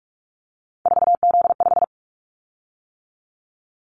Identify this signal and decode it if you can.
Morse code, heard as a keyed tone, about 44 wpm, 700 Hz. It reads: 4Z5
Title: Callsign Prefixes for Top 50 DXCC - 45wpm